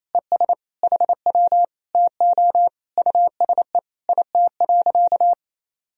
Morse code, heard as a keyed tone, about 28 wpm, 700 Hz. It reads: ES HW TO USE IT.
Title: Sentences from Top 500 Words and Abbreviations 28wpm